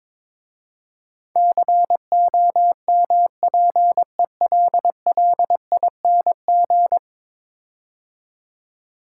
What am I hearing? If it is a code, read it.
Morse code, heard as a keyed tone, about 22 wpm, 700 Hz. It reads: COMPELLING
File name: practice / 22wpm / 10 Letter Words - First 3 Letters Given - 22wpm